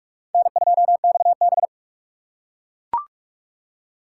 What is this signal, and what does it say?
Morse code, heard as a keyed tone, about 44 wpm, 700 Hz. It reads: N2XB
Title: USA Call Signs - 4 Characters - 6x Character Spacing - 45wpm